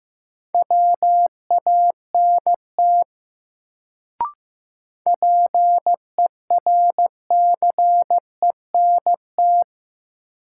Morse code, heard as a keyed tone, about 16 wpm, 700 Hz. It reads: WANT PERCENT
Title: Sets of 2 Words - New Word Drill - 15wpm